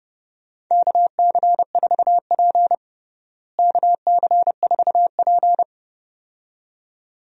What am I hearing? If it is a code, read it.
Morse code, heard as a keyed tone, about 32 wpm, 700 Hz. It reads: KC4P KC4P
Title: USA Call Signs - 4 Characters - Repeated 2x - 30wpm